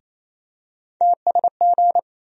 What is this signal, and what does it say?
Morse code, heard as a keyed tone, about 28 wpm, 700 Hz. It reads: TSG
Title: USA Call Sign Postfixes - 3 Letters 28wpm